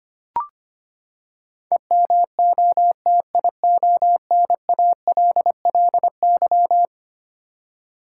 Morse code, heard as a keyed tone, about 25 wpm, 700 Hz. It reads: EMOTIONALLY
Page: morse-code-ninja-practice-25wpm